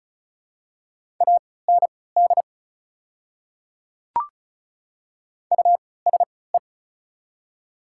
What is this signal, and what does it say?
Morse code, heard as a keyed tone, about 34 wpm, spaced as for 20 wpm, 690 Hz. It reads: AND USE
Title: 3x Character Spacing - Top 100 Words - 35wpm